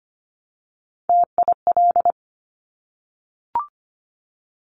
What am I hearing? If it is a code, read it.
Morse code, heard as a keyed tone, about 25 wpm, 700 Hz. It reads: TIL
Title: Top Three Letter Combinations in Words 25wpm